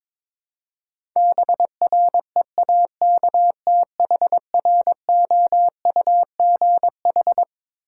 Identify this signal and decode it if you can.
Morse code, heard as a keyed tone, about 22 wpm, 700 Hz. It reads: BREAKTHROUGH
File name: Twelve Letter Words - 22wpm